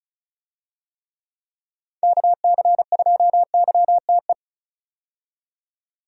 Morse code, heard as a keyed tone, about 36 wpm, 700 Hz. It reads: KC2YTE
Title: USA Call Signs - 6 Characters - 35wpm